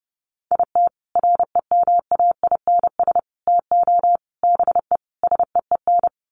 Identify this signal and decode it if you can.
Morse code, heard as a keyed tone, about 29 wpm, 700 Hz. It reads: IT REMAINS TO BE SEEN
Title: Sets of 5 Words Encore 30wpm